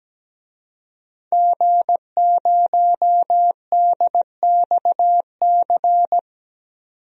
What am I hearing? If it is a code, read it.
Morse code, heard as a keyed tone, about 17 wpm, 700 Hz. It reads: G0DXC
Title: DXCC-100 Call Signs - 5 Characters - 17wpm